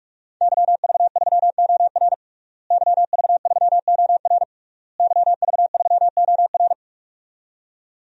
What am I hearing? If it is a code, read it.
Morse code, heard as a keyed tone, about 41 wpm, 700 Hz. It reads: YV3OR YV3OR YV3OR
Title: DXCC-100 Call Signs - 5 Characters - Repeated 3x - 45wpm